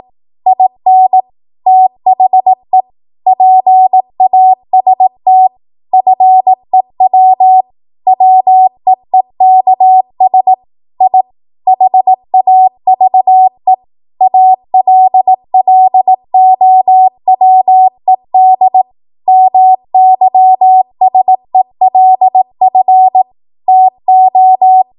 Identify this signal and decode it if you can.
Morse code, heard as a keyed tone, about 18 wpm, 750 Hz. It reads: IN THE PAST FEW WEEKS I HAVE ALLOWED MYSELF TO
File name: Morse / 18WPM